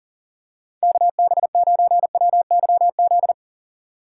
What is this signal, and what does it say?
Morse code, heard as a keyed tone, about 40 wpm, 700 Hz. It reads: KB9WYZ
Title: USA Call Signs - 2x Character Spacing - 40wpm